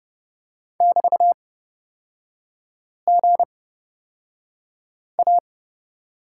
Morse code, heard as a keyed tone, about 30 wpm, spaced as for 5 wpm, 690 Hz. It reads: <BT>GA
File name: Lesson 37e - Letter x - Three Characters 30wpm